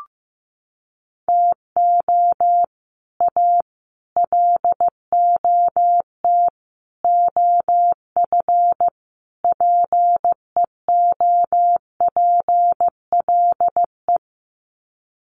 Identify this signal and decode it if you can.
Morse code, heard as a keyed tone, about 15 wpm, 700 Hz. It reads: TO A LOT OF PEOPLE
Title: Sets of 5 Words Encore 15wpm